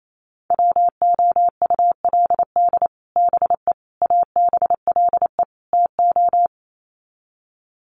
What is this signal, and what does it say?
Morse code, heard as a keyed tone, about 28 wpm, 700 Hz. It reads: WOULD BE ABLE TO
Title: Sets of 4 Words 28wpm